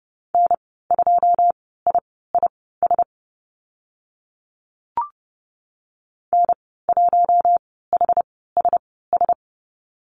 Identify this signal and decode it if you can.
Morse code, heard as a keyed tone, about 30 wpm, spaced as for 17 wpm, 700 Hz. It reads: N2IIS N1HSS
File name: Lesson 15g - Number 5 - Call Signs 30wpm